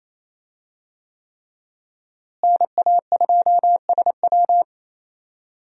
Morse code, heard as a keyed tone, about 28 wpm, 700 Hz. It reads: NA2SW
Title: USA Call Signs - 5 Characters - Repeated 3x - 28wpm